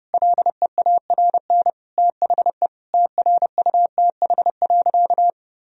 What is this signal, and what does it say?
Morse code, heard as a keyed tone, about 30 wpm, 700 Hz. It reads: LEARN THE TRUTH.
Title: Sentences from Top 1000 Words - Review Part 4 - 30wpm